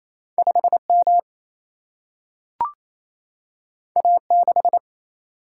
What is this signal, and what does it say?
Morse code, heard as a keyed tone, about 28 wpm, 700 Hz. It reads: 5 M A 6
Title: Single-Letter Single-Number 28wpm